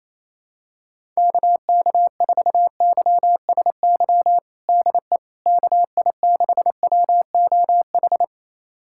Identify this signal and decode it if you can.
Morse code, heard as a keyed tone, about 27 wpm, 700 Hz. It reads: KK4YSY DE KI6WOH
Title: USA Callsign DE Callsign 28wpm